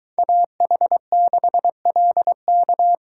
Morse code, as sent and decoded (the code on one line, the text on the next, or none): .- .... -.... .-.. -.-
AH6LK